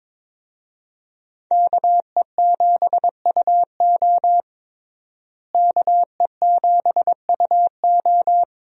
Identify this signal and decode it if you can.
Morse code, heard as a keyed tone, about 22 wpm, 700 Hz. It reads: KE7UO KE7UO
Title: USA Call Signs - 5 Characters - Repeated 3x - 22wpm